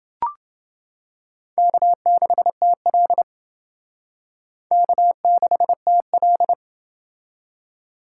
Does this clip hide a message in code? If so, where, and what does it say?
Yes: Morse code from 1.6 to 6.5 s, heard as a keyed tone, about 28 wpm, 700 Hz. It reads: K6TL K6TL